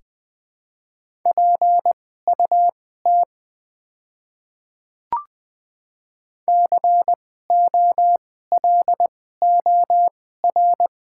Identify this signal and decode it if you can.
Morse code, heard as a keyed tone, about 20 wpm, spaced as for 15 wpm, 700 Hz. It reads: PUT COLOR
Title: 2x Character Spacing - Top 300 Words - 20wpm